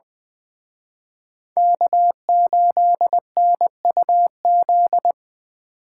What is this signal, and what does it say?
Morse code, heard as a keyed tone, about 23 wpm, 700 Hz. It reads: K8NUZ
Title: Speed Racing - USA 1x3 Call Signs 20wpm